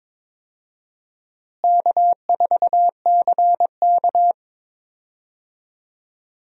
Morse code, heard as a keyed tone, about 22 wpm, 700 Hz. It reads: K4CK
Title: USA Call Signs - 4 Characters - Repeated 2x - 22wpm